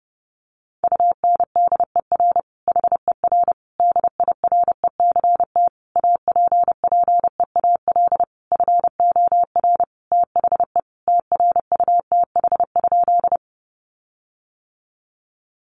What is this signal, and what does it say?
Morse code, heard as a keyed tone, about 28 wpm, 700 Hz. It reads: UNDER HER DIRECT APPEAL FOR THE TRUTH?
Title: Sentences from Top 1300 Words 30wpm